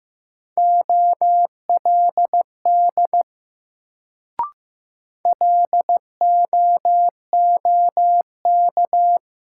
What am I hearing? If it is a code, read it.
Morse code, heard as a keyed tone, about 15 wpm, 700 Hz. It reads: OLD LOOK
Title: Sets of 2 Words - New Word Drill - 15wpm